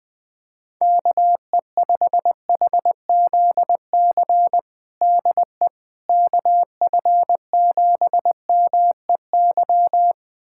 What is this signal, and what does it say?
Morse code, heard as a keyed tone, about 20 wpm, 700 Hz. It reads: KE5HZC DE KF7MEY
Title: USA Callsign DE Callsign - 6 Characters - 20wpm